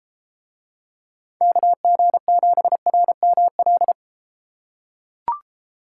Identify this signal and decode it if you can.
Morse code, heard as a keyed tone, about 33 wpm, 700 Hz. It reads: KG7RML